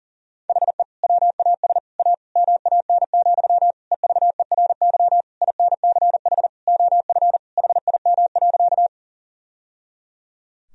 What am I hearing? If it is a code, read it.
Morse code, heard as a keyed tone, about 40 wpm, 700 Hz. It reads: HE WAS A MAN, EVERY INCH OF HIM.
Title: Sentences from Top 1000 Words - Review Part 3 - 40wpm